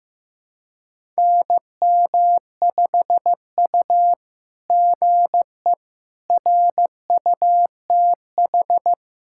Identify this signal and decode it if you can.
Morse code, heard as a keyed tone, about 15 wpm, 700 Hz. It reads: NM5U GE RUTH
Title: SOTA QSO Practice 15wpm